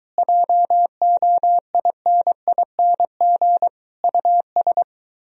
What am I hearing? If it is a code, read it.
Morse code, heard as a keyed tone, about 22 wpm, 700 Hz. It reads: JOINING US